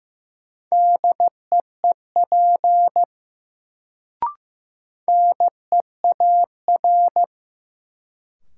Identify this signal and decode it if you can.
Morse code, heard as a keyed tone, about 15 wpm, 700 Hz. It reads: DEEP NEAR
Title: Sets of 5 Words Encore - New Word Drill - 15wpm